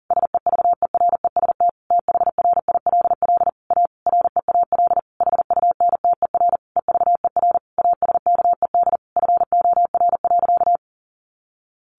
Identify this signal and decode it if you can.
Morse code, heard as a keyed tone, about 40 wpm, 700 Hz. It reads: SEVEREST THRILL A REAL HUNTER EVER ASKED FOR.
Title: A Sound Of Thunder 40wpm